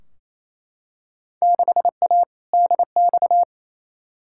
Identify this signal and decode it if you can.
Morse code, heard as a keyed tone, about 28 wpm, 700 Hz. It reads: 6A DX
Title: Field Day Exchanges Encore - 28wpm